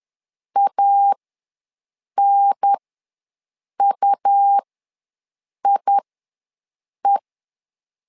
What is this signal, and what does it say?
Morse code, heard as a keyed tone, about 10 wpm, spaced as for 6 wpm, 800 Hz. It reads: ANUIE
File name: LEZIONI / 2 UHDN / 2 UHDN ws 10wpm 2044